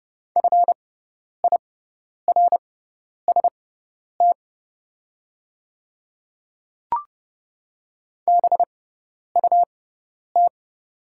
Morse code, heard as a keyed tone, about 30 wpm, spaced as for 10 wpm, 700 Hz. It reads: FIRST BUT
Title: 6x Character Spacing - Top 100 Words - 30wpm